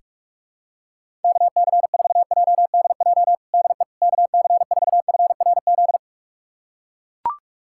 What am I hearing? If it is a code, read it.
Morse code, heard as a keyed tone, about 41 wpm, 700 Hz. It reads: KC4JDJ DE KC4FRZ
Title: USA Callsign DE Callsign - 6 Characters - 45wpm